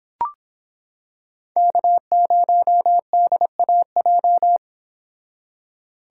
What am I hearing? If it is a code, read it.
Morse code, heard as a keyed tone, about 26 wpm, 700 Hz. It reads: K0DAJ